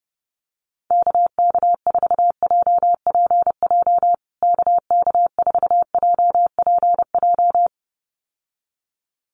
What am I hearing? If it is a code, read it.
Morse code, heard as a keyed tone, about 30 wpm, 700 Hz. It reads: KK4JPJ KK4JPJ
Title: USA Call Signs Repeated 2x 30wpm